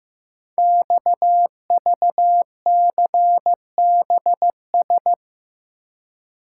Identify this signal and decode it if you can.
Morse code, heard as a keyed tone, about 15 wpm, 700 Hz. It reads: XVCBS
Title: Code Groups - 5 Letters - 15wpm